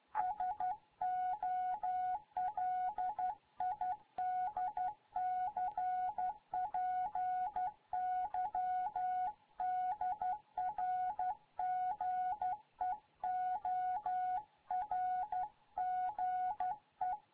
Morse code, heard as a keyed tone, about 12 wpm, 740 Hz. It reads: SOLID CPY DR GEORGE
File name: Morze-HA2MN